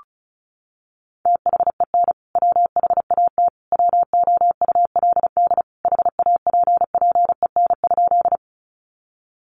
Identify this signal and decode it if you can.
Morse code, heard as a keyed tone, about 34 wpm, 700 Hz. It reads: THEN WHAT WOULD HAPPEN?
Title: Sentences from Top 400 Words 35wpm